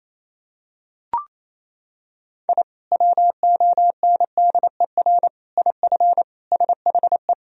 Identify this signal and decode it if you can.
Morse code, heard as a keyed tone, about 27 wpm, 700 Hz. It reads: I WONDER IF SHE
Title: Sentences from Top 1000 Words - Review Part 3 - 28wpm